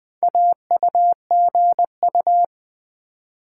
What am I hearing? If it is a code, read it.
Morse code, heard as a keyed tone, about 20 wpm, 700 Hz. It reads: AUGU
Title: Build A Word - 2 Letters At A Time - 6-Letter Words - 20wpm